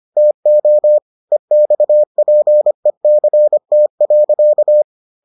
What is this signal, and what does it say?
Morse code, heard as a keyed tone, about 25 wpm, 600 Hz. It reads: TO EXPECT.